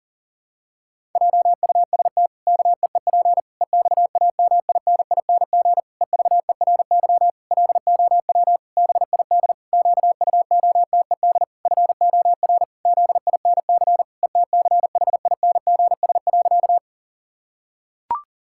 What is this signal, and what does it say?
Morse code, heard as a keyed tone, about 37 wpm, 700 Hz. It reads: JUST KEEP EXAMINING EVERY LOW BID QUOTED FOR ZINC ETCHINGS.
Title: Pangram Challenge v2 - 40wpm